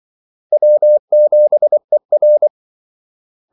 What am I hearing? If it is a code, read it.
Morse code, heard as a keyed tone, about 24 wpm, 600 Hz. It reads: W7ER
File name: Callsigns-24wpm